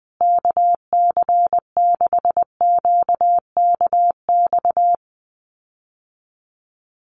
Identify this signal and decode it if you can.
Morse code, heard as a keyed tone, about 20 wpm, 700 Hz. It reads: KC6QKX
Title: USA Call Signs Repeated 3x 20wpm